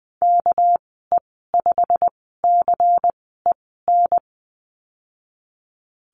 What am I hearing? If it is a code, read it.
Morse code, heard as a keyed tone, about 20 wpm, spaced as for 15 wpm, 700 Hz. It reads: KE5CEN
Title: 238 Lesson 38g - Letter q - Call Signs 20wpm